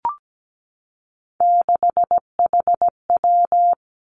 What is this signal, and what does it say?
Morse code, heard as a keyed tone, about 17 wpm, 700 Hz. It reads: 6HW
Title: Three Challenging Letter-Number Combinations 17wpm